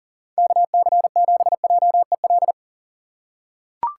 Morse code, heard as a keyed tone, about 40 wpm, 700 Hz. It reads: KC7JEL
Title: USA Call Signs - 6 Characters - 4x Character Spacing - 40wpm